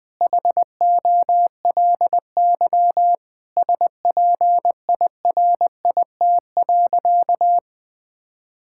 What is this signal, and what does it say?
Morse code, heard as a keyed tone, about 20 wpm, 700 Hz. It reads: HOLY SPIRIT.
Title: Sentences with 9-Word Phrases - Spoken Beginning - 20wpm